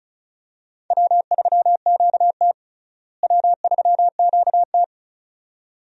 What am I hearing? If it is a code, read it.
Morse code, heard as a keyed tone, about 34 wpm, 700 Hz. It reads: W3QT W3QT
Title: USA Call Signs - 4 Characters - Repeated 2x - 35wpm